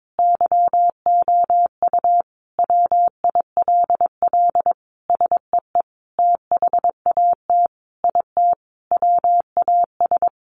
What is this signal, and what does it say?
Morse code, heard as a keyed tone, about 22 wpm, 700 Hz. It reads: YOU WILL SEE THAT IT WAS
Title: Sentences from Top 800 Words 22wpm